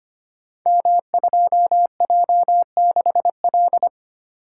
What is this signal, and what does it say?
Morse code, heard as a keyed tone, about 25 wpm, 700 Hz. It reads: M2J6L
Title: Code Groups - 5 Letter:Numbers - 25wpm